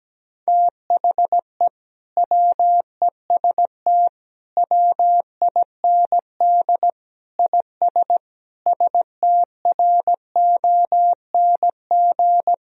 Morse code, heard as a keyed tone, about 17 wpm, 700 Hz. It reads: THE WEST WIND IS STRONG
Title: Sentences from Top 1000 Words - Review Part 2 - 17wpm